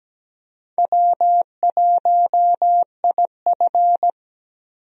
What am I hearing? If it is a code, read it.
Morse code, heard as a keyed tone, about 17 wpm, 700 Hz. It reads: W1IF
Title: USA Call Signs - 4 Characters - Repeated 3x - 17wpm